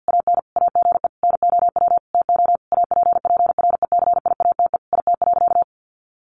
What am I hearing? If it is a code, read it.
Morse code, heard as a keyed tone, about 50 wpm, 700 Hz. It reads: AN AGE NOW TO APPRECIATE IT.
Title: Sentences from Top 1300 Words 50wpm